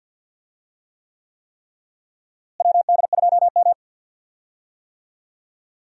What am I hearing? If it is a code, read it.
Morse code, heard as a keyed tone, about 49 wpm, 700 Hz. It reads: WD2M
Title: USA Call Signs - 4 Characters - Repeated 3x - 50wpm